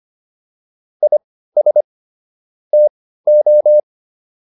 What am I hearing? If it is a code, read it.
Morse code, heard as a keyed tone, about 25 wpm, spaced as for 15 wpm, 600 Hz. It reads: IS TO